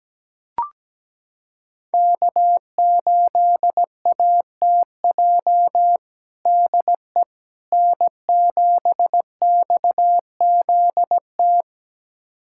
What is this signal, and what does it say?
Morse code, heard as a keyed tone, about 17 wpm, 700 Hz. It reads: K8ATJ DE N7XZT
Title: USA Callsign DE Callsign - 5 Characters - 17wpm